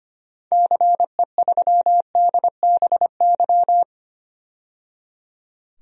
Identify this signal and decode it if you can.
Morse code, heard as a keyed tone, about 25 wpm, 700 Hz. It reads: CE3DBY
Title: DXCC-100 Call Signs - 6 Characters - 25wpm